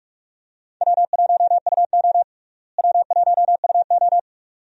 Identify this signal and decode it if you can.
Morse code, heard as a keyed tone, about 46 wpm, 700 Hz. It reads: W1UO W1UO
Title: USA Call Signs - 4 Characters - Repeated 2x - 45wpm